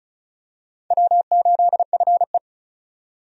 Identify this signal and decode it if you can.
Morse code, heard as a keyed tone, about 36 wpm, 700 Hz. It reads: W8FE
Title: USA Call Signs - 4 Characters - 35wpm